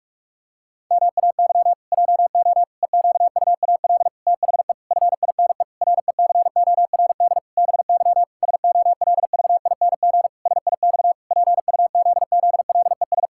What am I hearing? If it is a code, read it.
Morse code, heard as a keyed tone, about 46 wpm, 700 Hz. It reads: MAY JO EQUAL THE FINE RECORD BY SOLVING SIX PUZZLES
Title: Pangram Challenge v2 - 45wpm